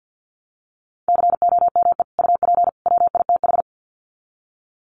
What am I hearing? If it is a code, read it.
Morse code, heard as a keyed tone, about 49 wpm, 700 Hz. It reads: COME UP WITH
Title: Sets of 3 Words 50wpm